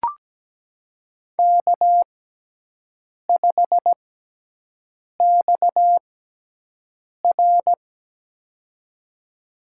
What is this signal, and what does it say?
Morse code, heard as a keyed tone, about 17 wpm, spaced as for 6 wpm, 700 Hz. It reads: K5XR
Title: USA Call Signs - 6x Character Spacing - 17wpm